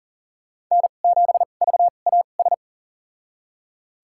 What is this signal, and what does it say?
Morse code, heard as a keyed tone, about 37 wpm, spaced as for 28 wpm, 700 Hz. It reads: N7VAS
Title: USA Call Signs - 5 Characters - 2x Character Spacing - 40wpm